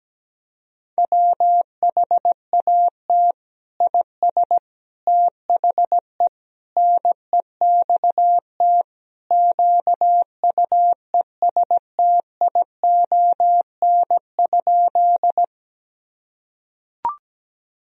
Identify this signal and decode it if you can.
Morse code, heard as a keyed tone, about 17 wpm, 700 Hz. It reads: WHAT IS THE NEXT QUESTION?
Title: Sentences from Top 1000 Words - Review Part 1 - 17wpm